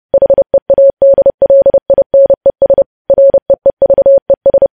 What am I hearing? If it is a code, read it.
Morse code, heard as a keyed tone, about 30 wpm, 550 Hz. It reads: HEADLINES REEVES
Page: news headlines in morse code at 30 wpm